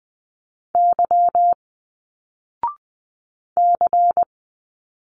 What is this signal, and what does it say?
Morse code, heard as a keyed tone, about 20 wpm, 700 Hz. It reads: Y C
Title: Single Letters 20wpm